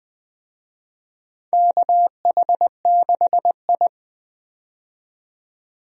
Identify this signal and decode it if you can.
Morse code, heard as a keyed tone, about 20 wpm, 700 Hz. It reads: KH6I
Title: DXCC-100 Call Signs - 4 Characters - 20wpm